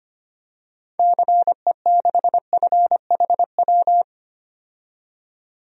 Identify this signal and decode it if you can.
Morse code, heard as a keyed tone, about 25 wpm, 700 Hz. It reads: CE6FHW
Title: DXCC-100 Call Signs - 6 Characters - 25wpm